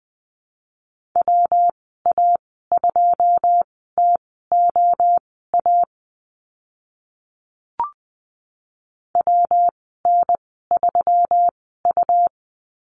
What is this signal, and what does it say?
Morse code, heard as a keyed tone, about 20 wpm, spaced as for 14 wpm, 700 Hz. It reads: WA2TOA WN3U
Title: 128 Lesson 21g - Number 6 - Call Signs 20wpm